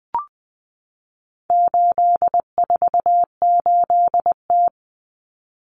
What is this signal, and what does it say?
Morse code, heard as a keyed tone, about 20 wpm, 700 Hz. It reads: 848T
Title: Common Cut Numbers - 4-Digit 20wpm